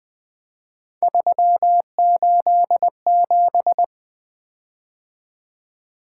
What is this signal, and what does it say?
Morse code, heard as a keyed tone, about 20 wpm, 700 Hz. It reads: 387
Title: Three Letter-Number Combinations - 20wpm